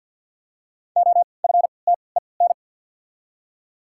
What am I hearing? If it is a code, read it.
Morse code, heard as a keyed tone, about 48 wpm, 700 Hz. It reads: O F T E N
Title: 3x Character Spacing - Top 300 Words - 50wpm